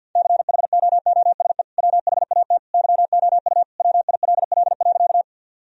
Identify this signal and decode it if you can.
Morse code, heard as a keyed tone, about 50 wpm, 700 Hz. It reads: CHOOSE WHAT YOU WILL.
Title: Sentences from Top 1000 Words - Review Part 4 - 50wpm